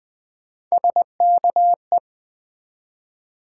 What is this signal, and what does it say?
Morse code, heard as a keyed tone, about 20 wpm, 700 Hz. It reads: SKE
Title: Build A Word - 3 Letters At A Time - 9-Letter Words - 20wpm